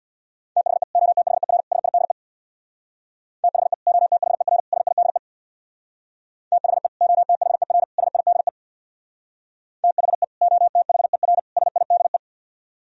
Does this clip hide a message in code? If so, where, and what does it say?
Yes: Morse code from 0.6 to 12.2 s, heard as a keyed tone, about 63 wpm, 700 Hz. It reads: THE OTHER SIDE THE OTHER SIDE THE OTHER SIDE THE OTHER SIDE